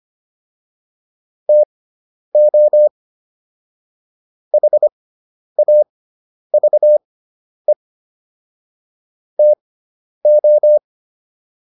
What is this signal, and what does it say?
Morse code, heard as a keyed tone, about 25 wpm, spaced as for 10 wpm, 600 Hz. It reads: TO HAVE TO